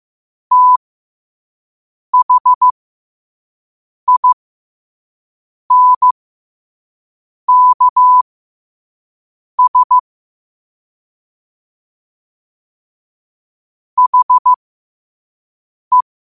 Morse code, heard as a keyed tone, about 15 wpm, spaced as for 5 wpm, 1000 Hz. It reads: THINKS HE
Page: learn-to-code